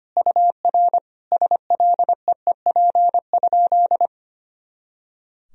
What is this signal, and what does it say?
Morse code, heard as a keyed tone, about 25 wpm, 700 Hz. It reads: UR SLEEP?
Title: Sentences from Top 500 Words and Abbreviations 25wpm